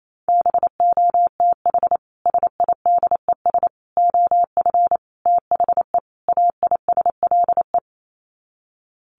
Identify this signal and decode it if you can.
Morse code, heard as a keyed tone, about 28 wpm, 700 Hz. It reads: BOTH SIDES OF THE AISLE
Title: Sets of 5 Words Encore2 28wpm